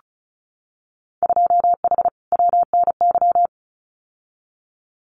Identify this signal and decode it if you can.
Morse code, heard as a keyed tone, about 34 wpm, 700 Hz. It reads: 2H WNY
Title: Winter Field Day Exchanges Encore 35wpm